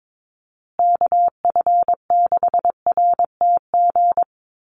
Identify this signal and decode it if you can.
Morse code, heard as a keyed tone, about 22 wpm, 700 Hz. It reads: KF6RTG
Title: USA Call Signs Repeated 3x 22wpm